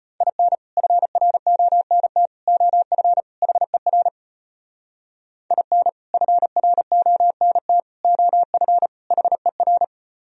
Morse code, heard as a keyed tone, about 32 wpm, 700 Hz. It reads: IN FRONT OF HER IN FRONT OF HER